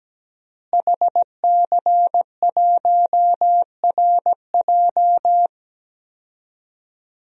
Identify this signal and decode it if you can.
Morse code, heard as a keyed tone, about 17 wpm, 700 Hz. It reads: HC1RJ
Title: DXCC-100 Call Signs - 5 Characters - Repeated 3x - 17wpm